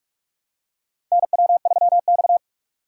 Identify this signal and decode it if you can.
Morse code, heard as a keyed tone, about 46 wpm, 700 Hz. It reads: NW3X
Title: USA Call Signs - 2x Character Spacing - 45wpm